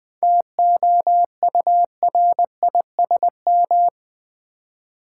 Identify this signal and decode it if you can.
Morse code, heard as a keyed tone, about 20 wpm, 690 Hz. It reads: TOURISM
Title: 7 Letter Words - First 3 Letters Given - 20wpm